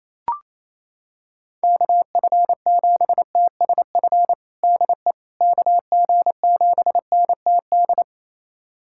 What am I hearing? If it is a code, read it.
Morse code, heard as a keyed tone, about 28 wpm, 700 Hz. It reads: KF7TSF DE KG7NTD
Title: USA Callsign DE Callsign - 6 Characters - 28wpm